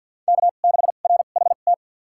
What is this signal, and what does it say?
Morse code, heard as a keyed tone, about 51 wpm, spaced as for 36 wpm, 700 Hz. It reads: K6RHT
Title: USA Call Signs - 5 Characters - 2x Character Spacing - 50wpm